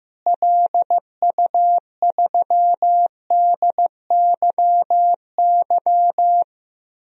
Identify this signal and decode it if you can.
Morse code, heard as a keyed tone, about 15 wpm, 700 Hz. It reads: LU3DYY
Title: DXCC-100 Call Signs - 6 Characters - Repeated 2x - 15wpm